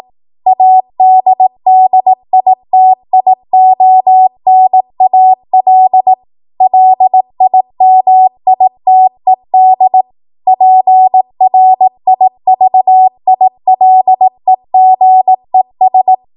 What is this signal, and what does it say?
Morse code, heard as a keyed tone, about 18 wpm, 750 Hz. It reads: ADDITIONAL LIMITED PRIVILEGES
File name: Morse / 18WPM